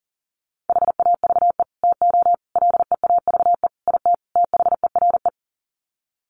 Morse code, heard as a keyed tone, about 40 wpm, 700 Hz. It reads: HAVE TO LEAVE IT THERE
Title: Sets of 5 Words Encore2 40wpm